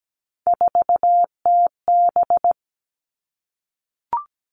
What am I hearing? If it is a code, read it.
Morse code, heard as a keyed tone, about 17 wpm, 700 Hz. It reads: 4TB
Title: All Cut Numbers - 3-Digit 17wpm